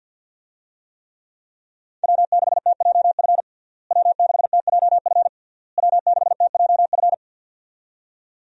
Morse code, heard as a keyed tone, about 52 wpm, 700 Hz. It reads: W6TJF W6TJF W6TJF
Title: USA Call Signs - 5 Characters - Repeated 3x - 50wpm